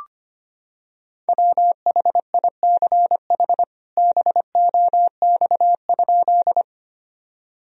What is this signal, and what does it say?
Morse code, heard as a keyed tone, about 25 wpm, 700 Hz. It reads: WHICH BOX?
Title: Sentences from Top 1000 Words - Review Part 2 - 25wpm